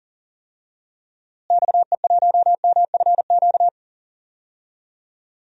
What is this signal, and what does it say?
Morse code, heard as a keyed tone, about 39 wpm, 700 Hz. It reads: XE1MFQ
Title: DXCC-100 Call Signs - 6 Characters - Repeated 2x - 40wpm